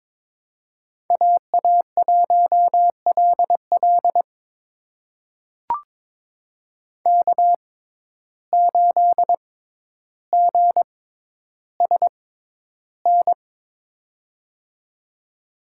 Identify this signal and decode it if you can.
Morse code, heard as a keyed tone, about 22 wpm, spaced as for 11 wpm, 700 Hz. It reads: AA1LL K8GSN
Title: USA Call Signs - 5 Characters - 6x Character Spacing - 22wpm